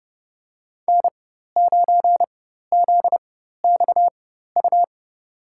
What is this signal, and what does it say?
Morse code, heard as a keyed tone, about 31 wpm, spaced as for 14 wpm, 700 Hz. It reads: N9ZXU
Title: USA Call Signs - 5 Characters - 4x Character Spacing - 30wpm